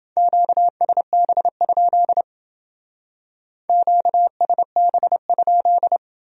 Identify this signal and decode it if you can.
Morse code, heard as a keyed tone, about 29 wpm, 700 Hz. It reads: QSB? QSB?